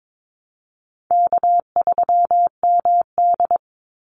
Morse code, heard as a keyed tone, about 22 wpm, 700 Hz. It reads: K3MD
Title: Easy Call Signs Repeated 3x - 22wpm